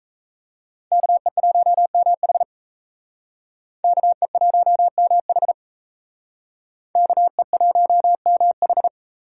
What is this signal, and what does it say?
Morse code, heard as a keyed tone, about 35 wpm, 700 Hz. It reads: KE1MH KE1MH KE1MH